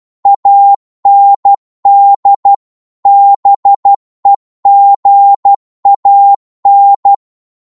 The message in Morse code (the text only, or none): AND BEGAN